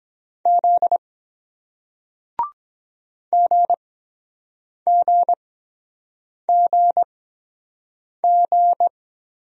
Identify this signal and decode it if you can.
Morse code, heard as a keyed tone, about 23 wpm, spaced as for 7 wpm, 700 Hz. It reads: Z GGGG